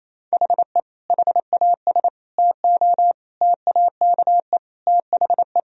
Morse code, heard as a keyed tone, about 28 wpm, 700 Hz. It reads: HE HAS TO TAKE THE
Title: Sentences from Top 1000 Words - Review Part 2 - 28wpm